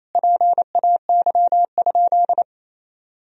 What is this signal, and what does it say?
Morse code, heard as a keyed tone, about 28 wpm, 700 Hz. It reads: PAY?
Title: Sentences from Top 1000 Words - Review Part 3 - 28wpm